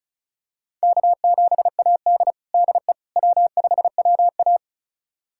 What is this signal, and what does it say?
Morse code, heard as a keyed tone, about 34 wpm, 700 Hz. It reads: K7AD DE W5WA
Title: USA Callsign DE Callsign - 4 Characters - 35wpm